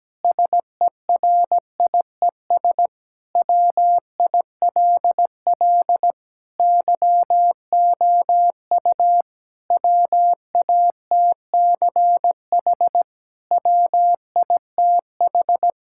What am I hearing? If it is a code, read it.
Morse code, heard as a keyed tone, about 17 wpm, 700 Hz. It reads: SERIES WILL YOU WATCH WITH
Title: Sentences from Top 1000 Words - Review Part 5 - 17wpm